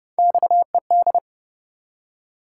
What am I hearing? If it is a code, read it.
Morse code, heard as a keyed tone, about 28 wpm, 700 Hz. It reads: XED